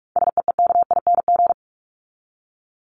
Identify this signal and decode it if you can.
Morse code, heard as a keyed tone, about 44 wpm, 700 Hz. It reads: SEEKING
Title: ING Suffix - Top 100 Words with Base Words 45wpm